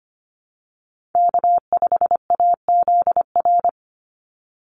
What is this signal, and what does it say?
Morse code, heard as a keyed tone, about 25 wpm, 700 Hz. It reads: K5AZR
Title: Call Signs Repeated 3x - 25wpm